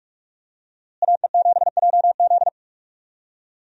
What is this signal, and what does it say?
Morse code, heard as a keyed tone, about 46 wpm, 700 Hz. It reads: AE7JZ
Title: USA Call Signs - 5 Characters - Repeated 3x - 45wpm